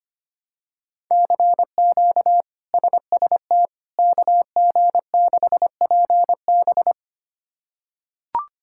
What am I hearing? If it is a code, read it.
Morse code, heard as a keyed tone, about 25 wpm, 700 Hz. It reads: CQ SST KG6PB
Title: K1USN SST QSO Practice - 25wpm